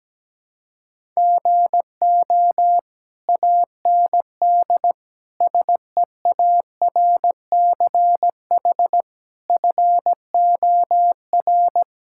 Morse code, heard as a keyed tone, about 17 wpm, 700 Hz. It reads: GO AND SEARCH FOR
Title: Sentences from Top 1000 Words - Review Part 5 - 17wpm